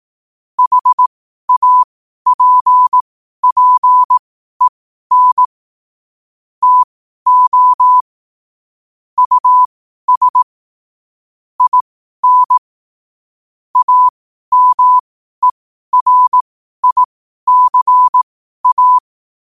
Morse code, heard as a keyed tone, about 18 wpm, spaced as for 12 wpm, 1000 Hz. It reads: HAPPEN TO US IN AMERICA
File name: qotdfiles / 12wpm